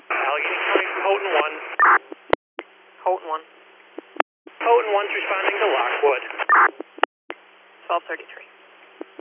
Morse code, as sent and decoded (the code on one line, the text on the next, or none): none